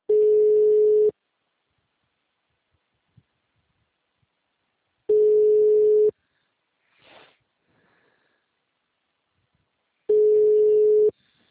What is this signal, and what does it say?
Ringback tone (Europe, 425 Hz).